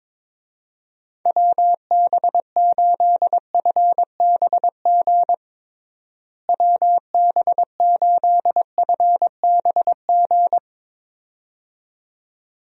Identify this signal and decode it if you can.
Morse code, heard as a keyed tone, about 23 wpm, 700 Hz. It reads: WB8FBG WB8FBG
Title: USA Call Signs - 6 Characters - Repeated 2x - 22wpm